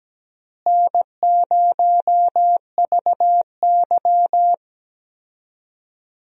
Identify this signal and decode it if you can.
Morse code, heard as a keyed tone, about 17 wpm, 700 Hz. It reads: N0VY
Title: USA Call Signs - 4 Characters - Repeated 2x - 17wpm